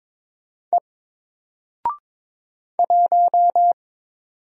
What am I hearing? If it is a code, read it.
Morse code, heard as a keyed tone, about 22 wpm, 700 Hz. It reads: E 1
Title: Single Letter-Number 22wpm